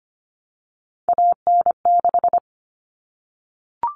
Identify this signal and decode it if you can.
Morse code, heard as a keyed tone, about 25 wpm, 700 Hz. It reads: AN6
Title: All Cut Numbers - 3-Digit 25wpm